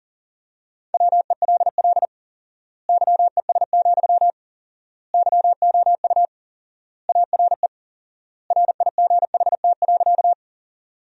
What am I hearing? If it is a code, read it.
Morse code, heard as a keyed tone, about 40 wpm, 700 Hz. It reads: WELL YES, YOU ARE RIGHT.
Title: 4x Word-Spacing - Sentences from Top 500 Words - 40wpm